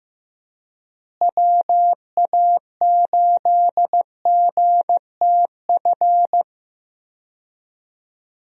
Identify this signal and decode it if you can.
Morse code, heard as a keyed tone, about 15 wpm, 700 Hz. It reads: WA8GTF
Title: USA Call Signs - 6 Characters - Repeated 3x - 15wpm